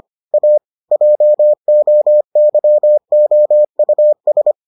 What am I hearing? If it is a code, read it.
Morse code, heard as a keyed tone, about 25 wpm, 600 Hz. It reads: A JOYOUS